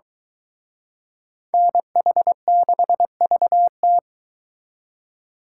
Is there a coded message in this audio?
Yes: Morse code from 1.5 to 4.0 s, heard as a keyed tone, about 23 wpm, 700 Hz. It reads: NH6VT